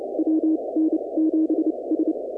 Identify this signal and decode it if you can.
Morse code, heard as a keyed tone, about 29 wpm, spaced as for 23 wpm, 320 Hz. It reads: WN7S
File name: contests / 2019-cw-wpx-kt5j-k5pi- -2330 0-20